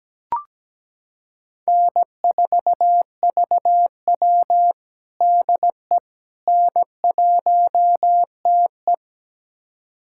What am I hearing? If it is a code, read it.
Morse code, heard as a keyed tone, about 17 wpm, 700 Hz. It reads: N4VW DE N1TE
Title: USA Callsign DE Callsign - 4 Characters - 17wpm